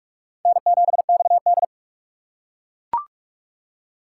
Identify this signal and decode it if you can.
Morse code, heard as a keyed tone, about 41 wpm, 700 Hz. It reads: N7XD
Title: USA Call Signs - 4 Characters - 4x Character Spacing - 45wpm